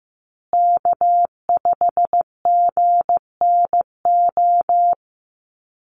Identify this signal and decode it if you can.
Morse code, heard as a keyed tone, about 15 wpm, 700 Hz. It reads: K5GNO
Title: USA Call Signs Repeated 2x 15wpm